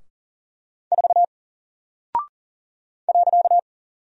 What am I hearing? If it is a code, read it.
Morse code, heard as a keyed tone, about 40 wpm, 700 Hz. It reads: 4 .
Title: Single Character (Letter:Number:Punctuation:Prosign) Alt 40wpm